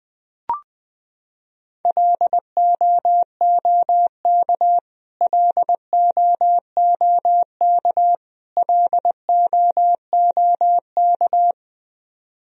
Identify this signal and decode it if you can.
Morse code, heard as a keyed tone, about 21 wpm, 700 Hz. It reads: LOOK LOOK LOOK
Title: Top 100 Words Repeated 20wpm